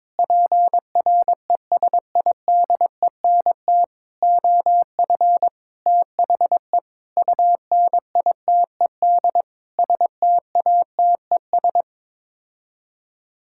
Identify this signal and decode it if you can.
Morse code, heard as a keyed tone, about 21 wpm, 700 Hz. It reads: PRESIDENT OF THE UNITED STATES
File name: Sets of 9 Words - 22wpm